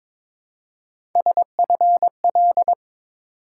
Morse code, heard as a keyed tone, about 22 wpm, 700 Hz. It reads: SFL
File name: ARRL Sections v2 - 22wpm